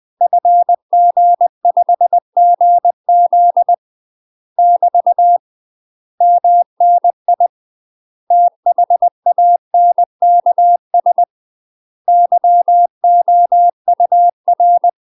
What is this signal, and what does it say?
Morse code, heard as a keyed tone, about 20 wpm, 700 Hz. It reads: FG5GZ <BT> MNI THANKS YOUR
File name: UK MASTER QSOs TRAINING / 20WPM / UK MASTER QSOs 0034